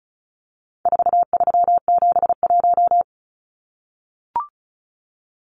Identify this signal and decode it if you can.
Morse code, heard as a keyed tone, about 34 wpm, 700 Hz. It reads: 4371
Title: Numbers - 4-Digit 35wpm